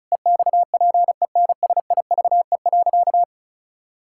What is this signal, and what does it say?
Morse code, heard as a keyed tone, about 36 wpm, 700 Hz. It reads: EXPENSIVE.
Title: Sentences from Top 1000 Words - Review Part 6 - 35wpm